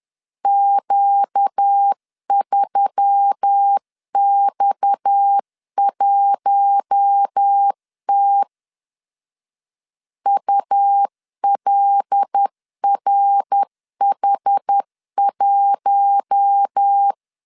Morse code, heard as a keyed tone, about 10 wpm, 800 Hz. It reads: Q3X1T ULRH1
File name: CW / DET / 1 6 1500C S C 10wpm 4252